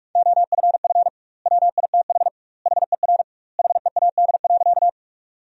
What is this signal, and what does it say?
Morse code, heard as a keyed tone, about 44 wpm, 700 Hz. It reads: OFF WITH HER HEAD.
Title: 2x Word-Spacing - Sentences from Top 300 Words - 45wpm